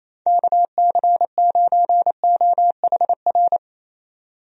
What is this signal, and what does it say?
Morse code, heard as a keyed tone, about 28 wpm, 700 Hz. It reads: KC9OHR
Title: USA Call Signs - 6 Characters - 28wpm